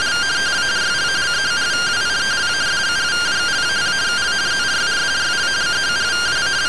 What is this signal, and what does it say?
Two-tone teleprinter signal, about 170 Hz shift.